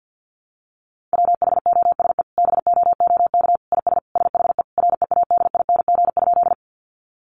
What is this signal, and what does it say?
Morse code, heard as a keyed tone, about 50 wpm, 700 Hz. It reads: WHOSE BOOK IS SHE READING?
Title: Sentences from Top 1200 Words 50wpm